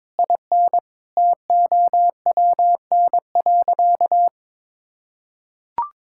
Morse code, heard as a keyed tone, about 22 wpm, 700 Hz. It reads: IN TOWN.
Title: Sentences from Top 1600 Words 22wpm